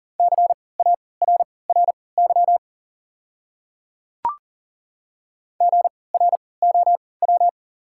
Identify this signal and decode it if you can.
Morse code, heard as a keyed tone, about 40 wpm, spaced as for 23 wpm, 700 Hz. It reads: CARRY GROW
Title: 3x Character Spacing - Top 300 Words - 40wpm